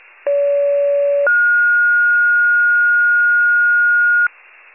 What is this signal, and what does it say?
Two-tone page, heard as a one-second tone, then a three-second tone, about 570 then 1430 Hz. It is heard from 0.3 to 4.3 s.
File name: pager / 29 FIRE / wcalert